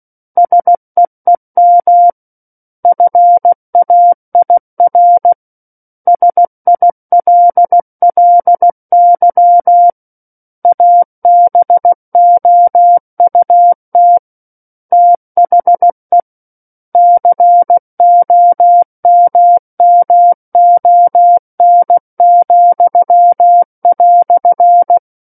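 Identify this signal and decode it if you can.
Morse code, heard as a keyed tone, about 16 wpm, 700 Hz. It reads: SEEM FAIR SILLY ABOUT THE COMMON,"
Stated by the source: Morse practice of book text at 16 wpm.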